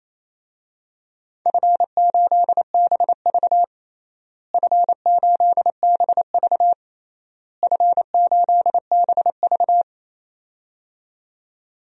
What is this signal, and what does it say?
Morse code, heard as a keyed tone, about 28 wpm, 700 Hz. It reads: F8BV F8BV F8BV
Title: DXCC-100 Call Signs - 4 Characters - Repeated 3x - 28wpm